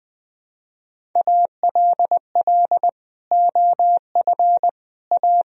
Morse code, heard as a keyed tone, about 21 wpm, 700 Hz. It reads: ALL OF A
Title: Speed-Racing - Sets of 4 Words 20wpm